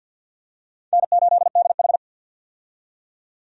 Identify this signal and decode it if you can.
Morse code, heard as a keyed tone, about 47 wpm, 700 Hz. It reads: N8DH
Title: USA Call Signs - 4 Characters - 50wpm